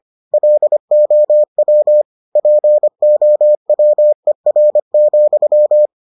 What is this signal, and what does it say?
Morse code, heard as a keyed tone, about 25 wpm, 600 Hz. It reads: LOW POWER,